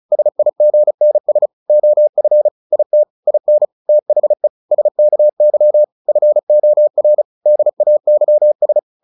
Morse code, heard as a keyed tone, about 34 wpm, 600 Hz. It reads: SIGNS OF IT IN THE SKY FOR DAYS
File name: the old man and the sea / the old man and the sea 1 35wpm